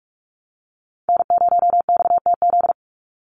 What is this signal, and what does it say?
Morse code, heard as a keyed tone, about 44 wpm, 700 Hz. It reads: N0XTZ
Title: USA Call Signs 45wpm